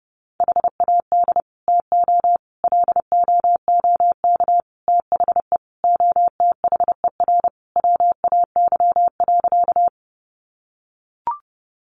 Morse code, heard as a keyed tone, about 30 wpm, 700 Hz. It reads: HAD TO LOOK THE OTHER WAY.
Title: Sentences from Top 100 Words 30wpm